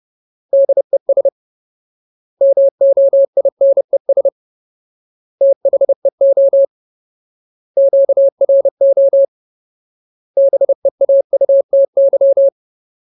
Morse code, heard as a keyed tone, about 30 wpm, 550 Hz. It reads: DES MOINES THEO QRO BEAUTY
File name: Ultimate Word List / 30WPM Ultimate